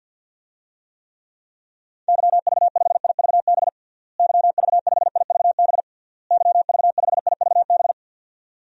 Morse code, heard as a keyed tone, about 52 wpm, 700 Hz. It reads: YV5IVB YV5IVB YV5IVB
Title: DXCC-100 Call Signs - 6 Characters - Repeated 3x - 50wpm